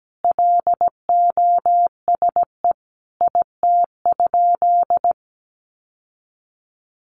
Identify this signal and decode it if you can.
Morse code, heard as a keyed tone, about 17 wpm, 700 Hz. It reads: LOSE IT?
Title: Sentences from Top 1100 Words 17wpm